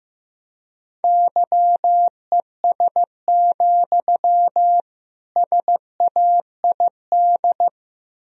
Morse code, heard as a keyed tone, about 15 wpm, 700 Hz. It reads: YES, SAID
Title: Sentences from Top 1000 Words - Review Part 4 - 15wpm